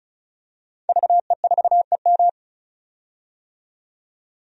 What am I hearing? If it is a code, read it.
Morse code, heard as a keyed tone, about 36 wpm, 700 Hz. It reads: VE4EM
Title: DXCC-100 Call Signs - 5 Characters - 35wpm